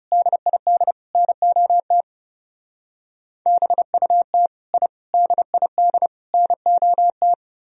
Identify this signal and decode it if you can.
Morse code, heard as a keyed tone, about 36 wpm, 700 Hz. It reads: DID NOT BUT I DID NOT
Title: Speed-Racing - Sets of 4 Words 30wpm